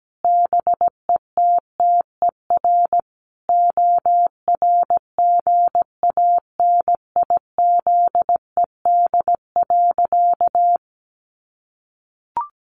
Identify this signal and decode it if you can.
Morse code, heard as a keyed tone, about 17 wpm, 700 Hz. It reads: BETTER ORGANIZED.
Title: Sentences from Top 1300 Words 17wpm